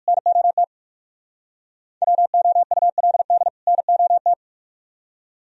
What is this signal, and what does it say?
Morse code, heard as a keyed tone, about 51 wpm, 700 Hz. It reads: NOT WOULD NOT
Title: Speed-Racing - Sets of 2 Words 45wpm